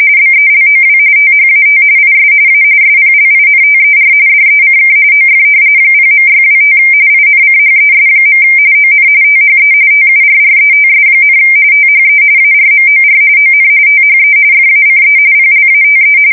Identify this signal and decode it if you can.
RTTY, about 45.45 baud. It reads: BELOW THAT IS A SPLINTER DECK OF 0.625 INCHES. ON THE TURRETS, THE ARMOR ON THE FACE PLATE
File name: fsk170 / battleshipnj